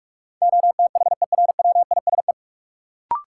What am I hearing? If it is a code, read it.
Morse code, heard as a keyed tone, about 41 wpm, 700 Hz. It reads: OTHERWISE
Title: Sentences from Top 1200 Words - New Word Drill Alt - 45wpm